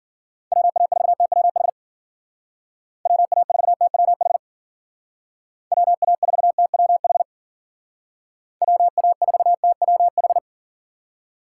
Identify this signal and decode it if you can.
Morse code, heard as a keyed tone, about 49 wpm, 700 Hz. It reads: WA4TWH WA4TWH WA4TWH WA4TWH
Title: Speed Racing - USA Call Signs 40wpm